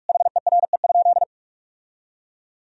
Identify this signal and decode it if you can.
Morse code, heard as a keyed tone, about 46 wpm, 700 Hz. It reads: HERE?
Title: Sentences from Top 1000 Words - Review Part 4 - 45wpm